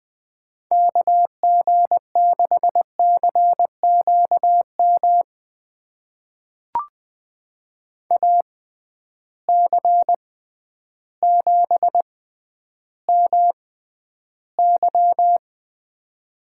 Morse code, heard as a keyed tone, about 20 wpm, spaced as for 11 wpm, 700 Hz. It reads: KG6CQM AC7MY
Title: USA Call Signs - 6x Character Spacing - 20wpm